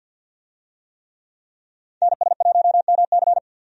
Morse code, heard as a keyed tone, about 51 wpm, 700 Hz. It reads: NS1MC
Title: USA Call Signs - 5 Characters - 50wpm